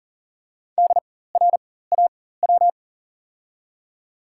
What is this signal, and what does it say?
Morse code, heard as a keyed tone, about 40 wpm, spaced as for 19 wpm, 700 Hz. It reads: DRAW
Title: 4x Character Spacing - Top 300 Words - 40wpm